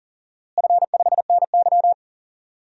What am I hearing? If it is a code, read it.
Morse code, heard as a keyed tone, about 40 wpm, 700 Hz. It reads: F5NY
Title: DXCC-100 Call Signs - 4 Characters - Repeated 2x - 40wpm